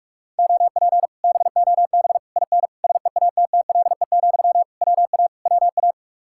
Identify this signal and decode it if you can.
Morse code, heard as a keyed tone, about 60 wpm, spaced as for 52 wpm, 700 Hz. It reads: OP BOB IN SEATTLE, WA WA
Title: Speed-Racing Common QSO Phrases Encore - 45wpm